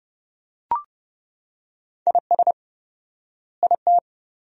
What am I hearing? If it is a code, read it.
Morse code, heard as a keyed tone, about 30 wpm, 700 Hz. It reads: IS IT
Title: 4x Word-Spacing - Sentences from Top 500 Words - 30wpm